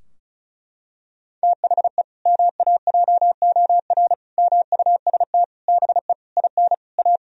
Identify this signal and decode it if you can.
Morse code, heard as a keyed tone, about 34 wpm, 700 Hz. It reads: THE MAJOR MUST BE IN A
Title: Sentences from Top 1000 Words - Review Part 4 - 35wpm